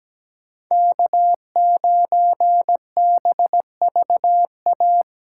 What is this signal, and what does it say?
Morse code, heard as a keyed tone, about 17 wpm, 700 Hz. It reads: K9BVA
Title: USA Callsign DE Callsign 17wpm